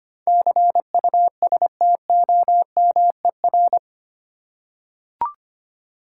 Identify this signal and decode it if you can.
Morse code, heard as a keyed tone, about 25 wpm, 700 Hz. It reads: CUSTOMER
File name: 8 Letter Words - First 3 Letters Given - 25wpm